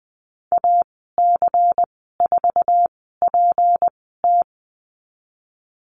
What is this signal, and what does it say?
Morse code, heard as a keyed tone, about 20 wpm, spaced as for 14 wpm, 700 Hz. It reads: AC4PT
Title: 161 Lesson 26g - Letter g - Call Signs 20wpm